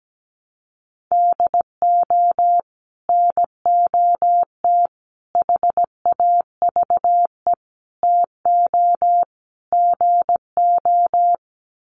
Morse code, heard as a keyed tone, about 17 wpm, 700 Hz. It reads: DO NOT HAVE TO GO
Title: Sets of 5 Words 17wpm